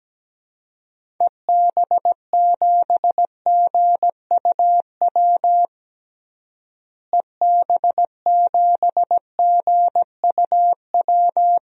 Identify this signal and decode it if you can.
Morse code, heard as a keyed tone, about 17 wpm, 700 Hz. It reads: EB7GUW EB7GUW
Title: DXCC-100 Call Signs - 6 Characters - Repeated 2x - 17wpm